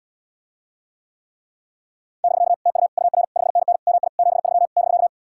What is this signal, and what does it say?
Morse code, heard as a keyed tone, about 80 wpm, 700 Hz. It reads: <BK> TU UR 5NN MI 73 <BK>